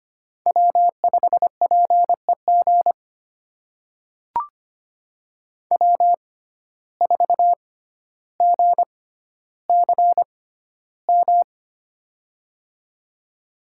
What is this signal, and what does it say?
Morse code, heard as a keyed tone, about 25 wpm, spaced as for 13 wpm, 700 Hz. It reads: W5PEG W4GCM
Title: USA Call Signs - 5 Characters - 6x Character Spacing - 25wpm